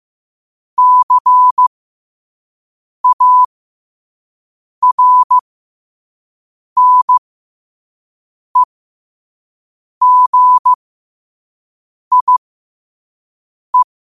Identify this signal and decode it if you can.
Morse code, heard as a keyed tone, about 15 wpm, spaced as for 5 wpm, 1000 Hz. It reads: CARNEGIE
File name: qotdfiles / 5wpm